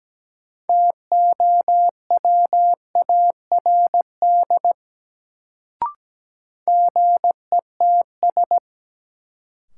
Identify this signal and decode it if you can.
Morse code, heard as a keyed tone, about 17 wpm, 700 Hz. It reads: TOWARD GETS
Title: Sentences from Top 1300 Words - New Word Drill Alt - 17wpm